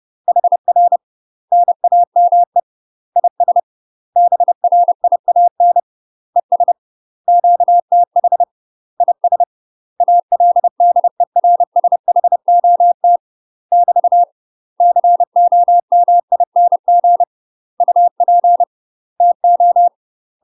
Morse code, heard as a keyed tone, about 30 wpm, 700 Hz. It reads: HR NAME IS BRIAN ES QTH IS ALDERSHOT <BT> COMING UP TO
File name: UK MASTER QSOs TRAINING / 30WPM / UK MASTER QSOs 0041